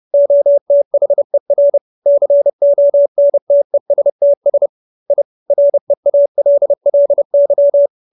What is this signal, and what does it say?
Morse code, heard as a keyed tone, about 30 wpm, 570 Hz. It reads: OTHER CONTESTS I REALLY